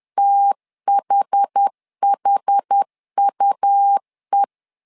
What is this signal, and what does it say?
Morse code, heard as a keyed tone, about 10 wpm, 800 Hz. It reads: THHUE
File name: LEZIONI / 2 UHDN / 2 UH 10wpm 1714